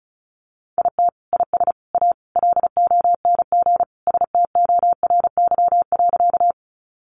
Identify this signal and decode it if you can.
Morse code, heard as a keyed tone, about 34 wpm, 700 Hz. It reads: IT IS A LONG STORY.
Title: Sentences from Top 300 Words 35wpm